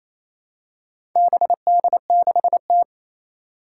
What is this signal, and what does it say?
Morse code, heard as a keyed tone, about 28 wpm, 700 Hz. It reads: BD6T
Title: Four Challenging Letter-Number Combinations 28wpm